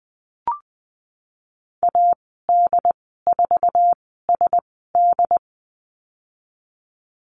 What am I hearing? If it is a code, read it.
Morse code, heard as a keyed tone, about 20 wpm, 700 Hz. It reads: A D 4 S D
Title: 076 Lesson 13g - Letter L - Call Signs 20wpm